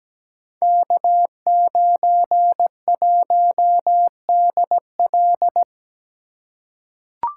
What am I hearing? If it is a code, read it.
Morse code, heard as a keyed tone, about 17 wpm, 700 Hz. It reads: K91DL
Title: Code Groups - 5 Letter:Numbers - 17wpm